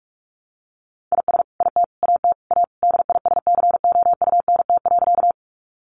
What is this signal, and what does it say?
Morse code, heard as a keyed tone, about 44 wpm, 700 Hz. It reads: IS IT AT A DISCOUNT.
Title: Sentences from Top 1200 Words 45wpm